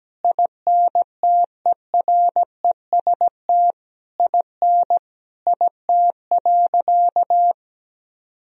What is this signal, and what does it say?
Morse code, heard as a keyed tone, about 17 wpm, 700 Hz. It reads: INTEREST IN IT.
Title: Sentences from Top 1000 Words - Review Part 4 - 17wpm